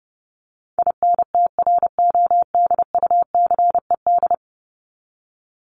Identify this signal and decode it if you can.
Morse code, heard as a keyed tone, about 30 wpm, 700 Hz. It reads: INTRODUCED
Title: ED Suffix - Top 100 Words with Base Words 30wpm